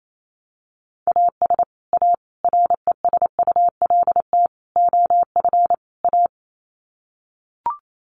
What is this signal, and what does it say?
Morse code, heard as a keyed tone, about 28 wpm, 700 Hz. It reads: AS A RESULT OF A
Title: Sets of 5 Words Encore 28wpm